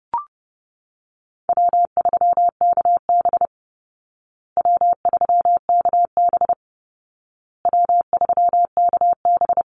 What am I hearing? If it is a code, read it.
Morse code, heard as a keyed tone, about 30 wpm, 700 Hz. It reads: W3KB W3KB W3KB
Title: Easy Call Signs Repeated 3x - 30wpm